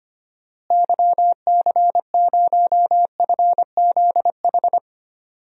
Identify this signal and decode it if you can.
Morse code, heard as a keyed tone, about 25 wpm, 700 Hz. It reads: YC0FZH
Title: DXCC-100 Call Signs - 6 Characters - Repeated 2x - 25wpm